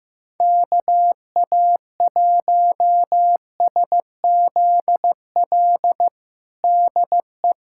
Morse code, heard as a keyed tone, about 15 wpm, 700 Hz. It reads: KA1SZL DE
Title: USA Callsign DE Callsign - 6 Characters - 15wpm